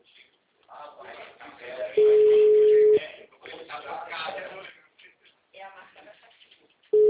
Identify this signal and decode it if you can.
Ringback tone (Europe, 425 Hz), with speech.